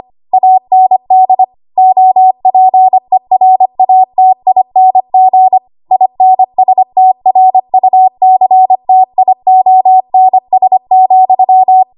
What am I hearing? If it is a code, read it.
Morse code, heard as a keyed tone, about 25 wpm, 750 Hz. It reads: AND OPERATING INSTRUCTIONS,
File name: Archive / 25 WPM / 25WPM